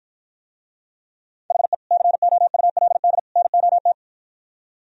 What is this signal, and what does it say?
Morse code, heard as a keyed tone, about 52 wpm, 690 Hz. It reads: HE COULD NOT